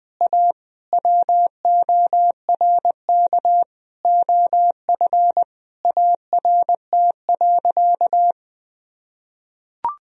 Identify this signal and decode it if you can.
Morse code, heard as a keyed tone, about 20 wpm, 700 Hz. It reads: A WORK OF ART.
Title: Sentences from Top 1000 Words - Review Part 3 - 20wpm